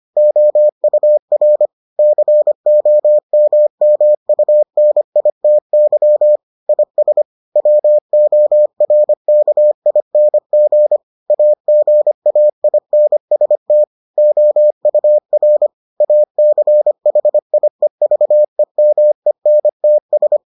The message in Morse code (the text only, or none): OUR COMMUNITY IS WORKING AGAINST OUR ACHIEVEMENTS